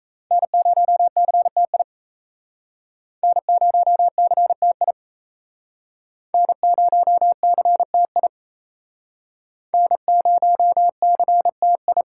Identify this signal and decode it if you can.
Morse code, heard as a keyed tone, about 38 wpm, 700 Hz. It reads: N0CTI N0CTI N0CTI N0CTI